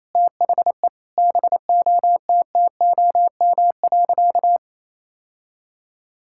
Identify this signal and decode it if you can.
Morse code, heard as a keyed tone, about 28 wpm, 700 Hz. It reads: THE BOTTOM.
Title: Sentences from Top 1000 Words - Review Part 6 - 28wpm